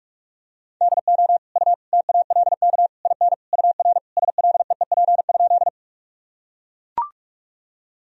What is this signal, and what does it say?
Morse code, heard as a keyed tone, about 41 wpm, 700 Hz. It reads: DO U TALK IN UR SLEEP?
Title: Sentences from Top 500 Words and Abbreviations 45wpm